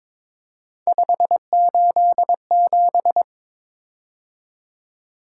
Morse code, heard as a keyed tone, about 22 wpm, 700 Hz. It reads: 587
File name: practice / 22wpm / Three Letter-Number Combinations - 22wpm